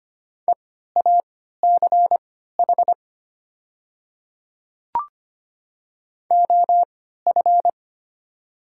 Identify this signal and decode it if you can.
Morse code, heard as a keyed tone, about 25 wpm, spaced as for 14 wpm, 700 Hz. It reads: EACH OF
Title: 3x Character Spacing - Top 100 Words - 25wpm